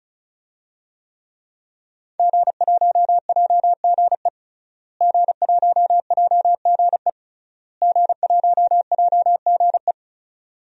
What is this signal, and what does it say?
Morse code, heard as a keyed tone, about 36 wpm, 700 Hz. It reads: G1JGE G1JGE G1JGE
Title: DXCC-100 Call Signs - 5 Characters - Repeated 3x - 35wpm